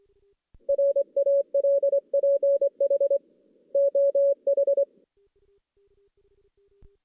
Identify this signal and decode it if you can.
Morse code, heard as a keyed tone, about 23 wpm, 560 Hz. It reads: RALPH OH
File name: 2016 JAN CW NAQP / 2016 Jan naqp k5ot - 0-5